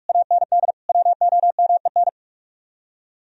Morse code, heard as a keyed tone, about 43 wpm, 700 Hz. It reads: AND WOMEN